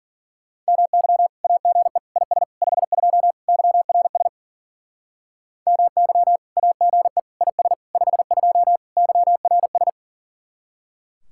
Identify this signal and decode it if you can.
Morse code, heard as a keyed tone, about 49 wpm, 700 Hz. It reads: MY AGE IS 52 YRS MY AGE IS 52 YRS
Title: Speed-Racing Common QSO Phrases - 40wpm